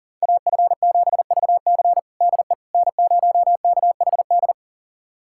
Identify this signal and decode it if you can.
Morse code, heard as a keyed tone, about 37 wpm, 700 Hz. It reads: AF7VC DE N0KHD
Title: USA Callsign DE Callsign - 5 Characters - 40wpm